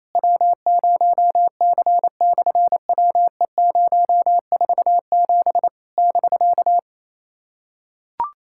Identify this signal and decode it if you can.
Morse code, heard as a keyed tone, about 28 wpm, 700 Hz. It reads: W0C/WE047 <BK>
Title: SOTA QSO Practice 28wpm